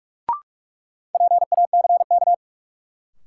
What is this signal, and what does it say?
Morse code, heard as a keyed tone, about 43 wpm, 700 Hz. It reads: PACK
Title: Sentences from Top 1300 Words - New Word Drill Alt - 45wpm